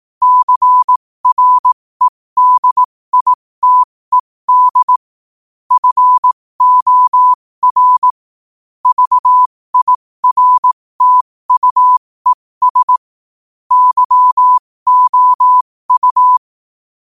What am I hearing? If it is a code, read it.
Morse code, heard as a keyed tone, about 18 wpm, spaced as for 15 wpm, 1000 Hz. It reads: CREDITED FOR VIRTUES YOU
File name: qotdfiles / 15wpm